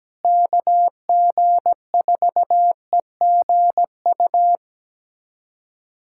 Morse code, heard as a keyed tone, about 17 wpm, 700 Hz. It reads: KG4EGU
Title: USA Callsign DE Callsign 17wpm